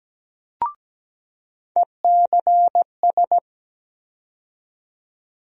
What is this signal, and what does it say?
Morse code, heard as a keyed tone, about 17 wpm, 700 Hz. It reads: ECS
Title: USA Call Sign Postfixes - 3 Letters 17wpm